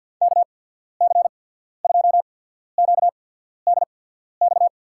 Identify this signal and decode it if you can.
Morse code, heard as a keyed tone, about 53 wpm, spaced as for 14 wpm, 700 Hz. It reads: KC2QDX
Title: USA Call Signs - 6 Characters - 8x Character Spacing - 50wpm